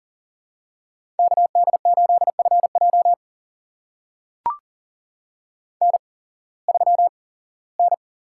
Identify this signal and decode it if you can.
Morse code, heard as a keyed tone, about 40 wpm, spaced as for 21 wpm, 700 Hz. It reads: KD8FJ N3N
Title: USA Call Signs - 5 Characters - 8x Character Spacing - 40wpm